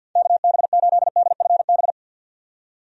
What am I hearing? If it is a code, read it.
Morse code, heard as a keyed tone, about 46 wpm, 700 Hz. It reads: KB8DFB
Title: USA Call Signs - 6 Characters - 50wpm